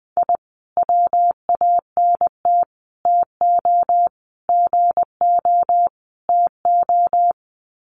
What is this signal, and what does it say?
Morse code, heard as a keyed tone, about 20 wpm, 700 Hz. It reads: I WANT TO GO TO
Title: Sets of 5 Words Encore 20wpm